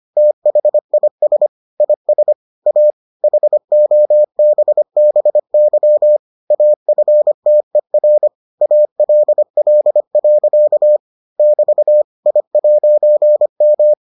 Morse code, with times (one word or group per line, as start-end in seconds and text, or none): 0.2-1.5 THIS
1.8-2.3 IS
2.7-2.9 A
3.2-6.2 HOBBY
6.5-8.3 AFTER
8.6-11.0 ALL.
11.4-12.0 <BT>
12.3-13.9 I'M